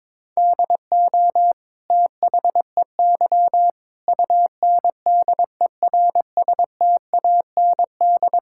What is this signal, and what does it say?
Morse code, heard as a keyed tone, about 22 wpm, 700 Hz. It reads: DO THEY UNDERSTAND
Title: Sentences from Top 1000 Words - Review Part 6 - 22wpm